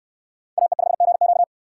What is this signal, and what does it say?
Morse code, heard as a keyed tone, about 65 wpm, 700 Hz. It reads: AE5GQ